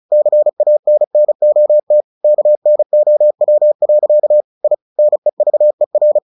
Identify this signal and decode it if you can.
Morse code, heard as a keyed tone, about 34 wpm, 600 Hz. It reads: CANNOT KNOW. I NEVER
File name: the old man and the sea 2 35wpm